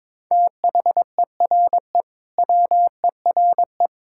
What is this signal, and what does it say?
Morse code, heard as a keyed tone, about 22 wpm, 700 Hz. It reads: THERE WERE
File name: practice / 22wpm / Sentences with 2-Word Phrases - 22wpm